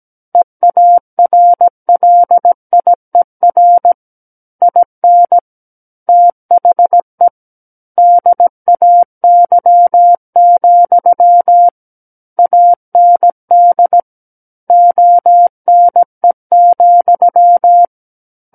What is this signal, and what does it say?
Morse code, heard as a keyed tone, about 17 wpm, 700 Hz. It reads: EARLIER IN THE DAY, AND ONE,
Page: SKCC Learning Center - Morse Code Practice Files